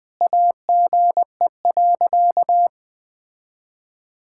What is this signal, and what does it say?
Morse code, heard as a keyed tone, about 19 wpm, 700 Hz. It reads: AGE.
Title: Sentences from Top 1000 Words - Review Part 6 - 20wpm